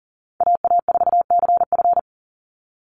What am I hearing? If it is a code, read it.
Morse code, heard as a keyed tone, about 40 wpm, 700 Hz. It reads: AA4CF
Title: Call Signs - 40wpm